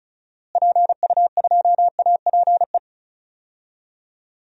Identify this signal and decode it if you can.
Morse code, heard as a keyed tone, about 36 wpm, 700 Hz. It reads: PU2APE
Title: DXCC-100 Call Signs - 6 Characters - Repeated 3x - 35wpm